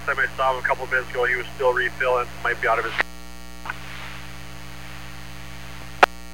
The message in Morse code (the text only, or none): none